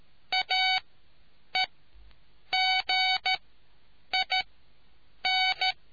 Morse code, heard as a keyed tone, about 13 wpm, spaced as for 8 wpm, 2250 Hz. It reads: AEGIN